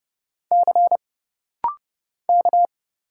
Morse code, heard as a keyed tone, about 30 wpm, 700 Hz. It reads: C K
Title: Single Letter-Number - Rapid-Fire 30wpm